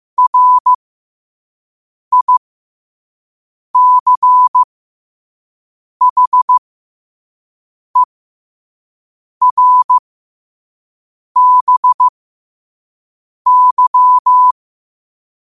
Morse code, heard as a keyed tone, about 15 wpm, spaced as for 5 wpm, 1000 Hz. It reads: RICHERBY
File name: 5wpm